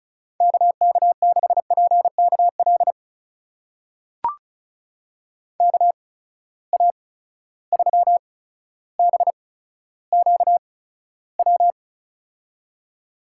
Morse code, heard as a keyed tone, about 35 wpm, spaced as for 15 wpm, 700 Hz. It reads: KK6PKL KA3BQW
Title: USA Call Signs - 6 Characters - 8x Character Spacing - 35wpm